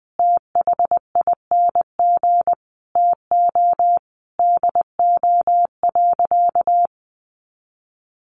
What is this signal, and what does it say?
Morse code, heard as a keyed tone, about 19 wpm, 700 Hz. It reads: THING TO DO.
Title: Sentences from Top 1200 Words 20wpm